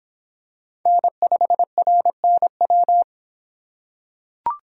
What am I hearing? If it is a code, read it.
Morse code, heard as a keyed tone, about 26 wpm, 700 Hz. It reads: N5RNW